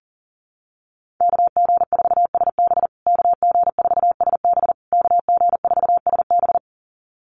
Morse code, heard as a keyed tone, about 40 wpm, 700 Hz. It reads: KG4SB KG4SB KG4SB
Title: USA Call Signs Repeated 3x 40wpm